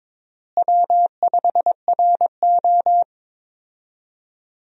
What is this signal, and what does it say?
Morse code, heard as a keyed tone, about 21 wpm, 700 Hz. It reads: W5RO
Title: USA Call Signs - 4 Characters - Repeated 2x - 22wpm